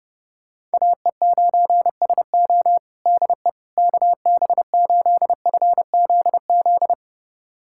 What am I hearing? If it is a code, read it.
Morse code, heard as a keyed tone, about 29 wpm, 700 Hz. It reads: AE9SO DE KB8FZZ
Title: USA Callsign DE Callsign 30wpm